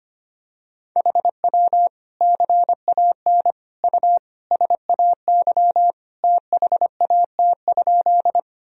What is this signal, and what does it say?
Morse code, heard as a keyed tone, about 25 wpm, 700 Hz. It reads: HW CAN U SAY THAT?
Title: Sentences from Top 200 Words and Abbreviations 25wpm